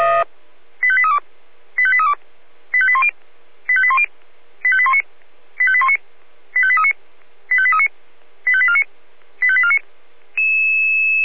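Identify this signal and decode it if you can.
Five-tone selective call (ZVEI), heard as quick runs of five tones, tones of about 70 ms. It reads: 85832 85832 85810 85810 85819 85819 85839 85839 85849 85849
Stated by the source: ZVEI five-tone selective call received by SDR.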